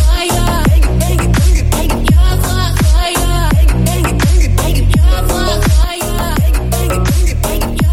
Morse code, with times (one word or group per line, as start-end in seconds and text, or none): none